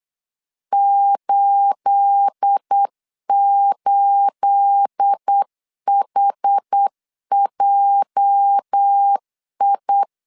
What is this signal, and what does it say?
Morse code, heard as a keyed tone, about 8 wpm, 800 Hz. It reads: 88HJI
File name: LEZIONI / 9 28 8wpm 3512